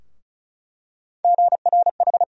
Morse code, heard as a keyed tone, about 36 wpm, 700 Hz. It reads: GRH
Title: USA Call Sign Postfixes - 3 Letters 35wpm